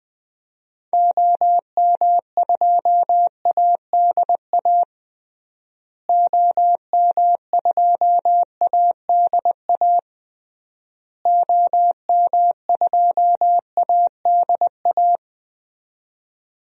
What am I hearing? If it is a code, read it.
Morse code, heard as a keyed tone, about 20 wpm, 700 Hz. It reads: OM2ADA OM2ADA OM2ADA
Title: DXCC-100 Call Signs - 6 Characters - Repeated 3x - 20wpm